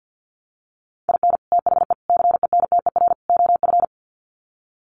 Morse code, heard as a keyed tone, about 50 wpm, 700 Hz. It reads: IN THE CENTER OF
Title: Sets of 4 Words 50wpm